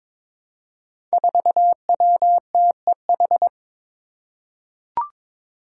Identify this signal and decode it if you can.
Morse code, heard as a keyed tone, about 22 wpm, 700 Hz. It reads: 4WTEH
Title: Five Challenging Letter-Number Combinations 22wpm